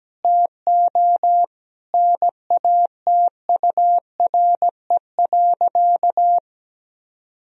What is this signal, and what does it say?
Morse code, heard as a keyed tone, about 17 wpm, 700 Hz. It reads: TO NATURE.
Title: Sentences from Top 1000 Words - Review Part 4 - 17wpm